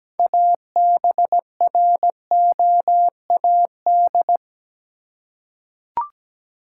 Morse code, heard as a keyed tone, about 17 wpm, 700 Hz. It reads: ABROAD
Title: Six Letter Words - 17wpm